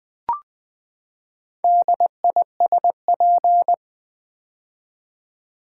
Morse code, heard as a keyed tone, about 20 wpm, 700 Hz. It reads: DISP
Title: Build A Word - 4 Letters At A Time - 8-Letter Words - 20wpm